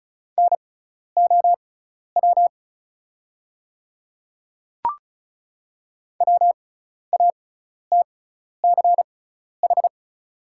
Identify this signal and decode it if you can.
Morse code, heard as a keyed tone, about 34 wpm, spaced as for 12 wpm, 690 Hz. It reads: NOW WATCH
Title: 6x Character Spacing - Top 500 Words - 35wpm